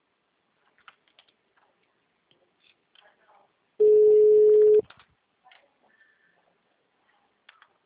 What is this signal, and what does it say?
Ringback tone (Europe, 425 Hz), with speech.
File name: out- -2003- - - .4